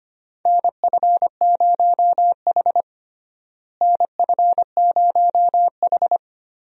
Morse code, heard as a keyed tone, about 25 wpm, 700 Hz. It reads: NF0H NF0H
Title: USA Call Signs - 4 Characters - Repeated 2x - 25wpm